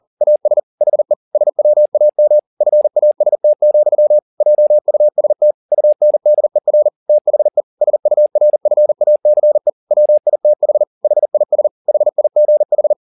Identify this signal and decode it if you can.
Morse code, heard as a keyed tone, about 40 wpm, 600 Hz. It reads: AS HE SWAM FAST, JUST UNDER THE SURFACE WITH HIS HIGH
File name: the old man and the sea 2 40wpm